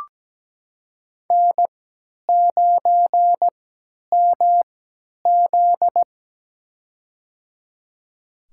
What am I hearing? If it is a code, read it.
Morse code, heard as a keyed tone, about 17 wpm, spaced as for 10 wpm, 700 Hz. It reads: N9MZ
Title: USA Call Signs - 4 Characters - 3x Character Spacing - 17wpm